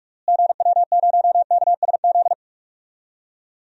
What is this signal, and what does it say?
Morse code, heard as a keyed tone, about 41 wpm, 700 Hz. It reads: GW0KSZ
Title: DXCC-100 Call Signs - 6 Characters - Repeated 3x - 45wpm